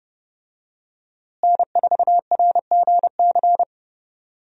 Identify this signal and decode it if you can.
Morse code, heard as a keyed tone, about 30 wpm, 700 Hz. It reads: N4RGC
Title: USA Call Signs - 5 Characters - Repeated 3x - 30wpm